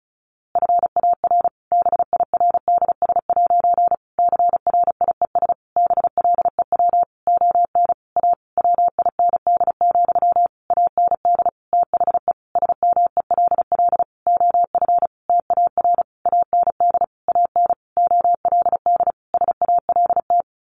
Morse code, heard as a keyed tone, about 34 wpm, 700 Hz. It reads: FAR BIRDS' CRIES BLEW ON A WIND, AND THE SMELL OF TAR AND AN OLD SALT
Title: A Sound Of Thunder 35wpm